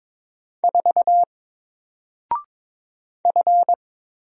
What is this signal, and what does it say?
Morse code, heard as a keyed tone, about 22 wpm, 700 Hz. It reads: 4 F
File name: Single-Letter-Number-22wpm